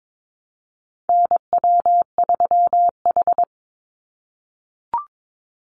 Morse code, heard as a keyed tone, about 22 wpm, 700 Hz. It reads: NW3H
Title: USA Call Signs 22wpm